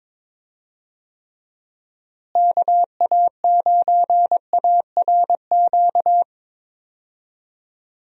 Morse code, heard as a keyed tone, about 22 wpm, 700 Hz. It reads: KA9ARQ
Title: USA Call Signs - 6 Characters - Repeated 2x - 22wpm